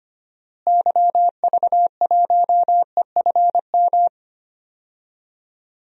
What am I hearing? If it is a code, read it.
Morse code, heard as a keyed tone, about 25 wpm, 700 Hz. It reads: YV1EFM
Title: DXCC-100 Call Signs - 6 Characters - Repeated 3x - 25wpm